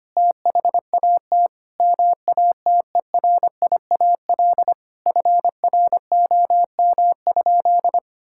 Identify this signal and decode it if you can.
Morse code, heard as a keyed tone, about 25 wpm, 700 Hz. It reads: THAT MATERIAL FROM?
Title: Sentences from Top 1000 Words - Review Part 5 - 25wpm